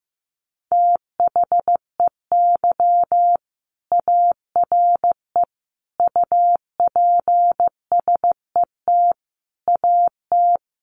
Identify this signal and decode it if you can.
Morse code, heard as a keyed tone, about 15 wpm, 700 Hz. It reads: THEY ARE UPSET AT
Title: Sentences from Top 800 Words 15wpm